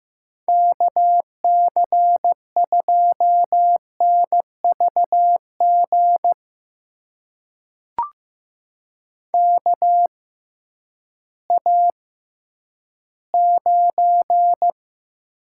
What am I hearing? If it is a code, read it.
Morse code, heard as a keyed tone, about 15 wpm, spaced as for 10 wpm, 700 Hz. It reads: KC2NVG KA9
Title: USA Call Signs - 6 Characters - 6x Character Spacing - 15wpm